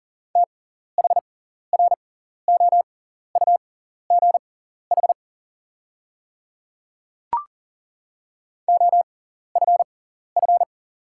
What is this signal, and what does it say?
Morse code, heard as a keyed tone, about 40 wpm, spaced as for 14 wpm, 690 Hz. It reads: THROUGH OFF
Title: 6x Character Spacing - Top 200 Words - 40wpm